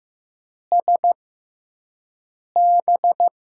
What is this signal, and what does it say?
Morse code, heard as a keyed tone, about 15 wpm, spaced as for 5 wpm, 700 Hz. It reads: SB
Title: USA Call Signs - 5 Characters - 6x Character Spacing - 15wpm